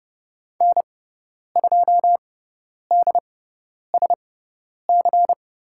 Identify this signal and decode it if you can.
Morse code, heard as a keyed tone, about 30 wpm, spaced as for 10 wpm, 690 Hz. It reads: N2DSC
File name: Lesson 18g - Letter m - Call Signs 30wpm